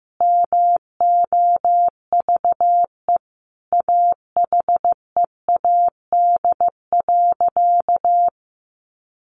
Morse code, heard as a keyed tone, about 15 wpm, 700 Hz. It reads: MOVE AHEAD.
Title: Sentences from Top 1100 Words 15wpm